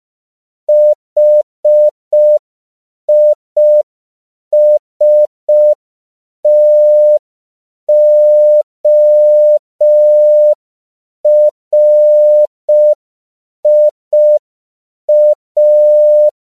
Morse code, heard as a keyed tone, about 5 wpm, 600 Hz. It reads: HISTORIA
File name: CW / 5PPM